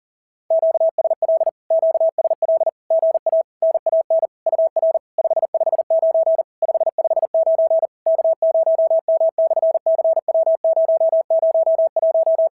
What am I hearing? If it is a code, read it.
Morse code, heard as a keyed tone, about 40 wpm, 650 Hz. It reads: QSL QSL GA NAN UR 559 559 K0M/CW001
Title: SOTA QSO Practice 40wpm